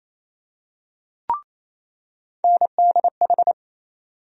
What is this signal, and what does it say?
Morse code, heard as a keyed tone, about 28 wpm, 700 Hz. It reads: NDH
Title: USA Call Sign Postfixes - 3 Letters 28wpm